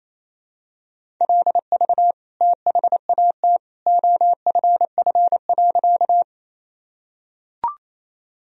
Morse code, heard as a keyed tone, about 28 wpm, 700 Hz. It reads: LV THAT OFF.
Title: Sentences from Top 400 Words and Abbreviations 28wpm